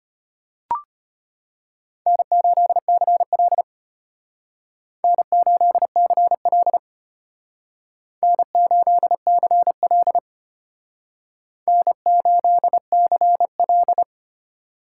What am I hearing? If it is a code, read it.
Morse code, heard as a keyed tone, about 31 wpm, 700 Hz. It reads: N8CL N8CL N8CL N8CL
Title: Speed Racing - USA Call Signs 25wpm